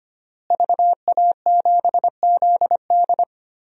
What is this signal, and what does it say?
Morse code, heard as a keyed tone, about 25 wpm, 700 Hz. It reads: VA7ZD
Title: DXCC-100 Call Signs - 5 Characters - Repeated 3x - 25wpm